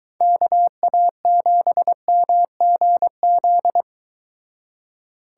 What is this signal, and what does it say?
Morse code, heard as a keyed tone, about 23 wpm, 700 Hz. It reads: KA7MGZ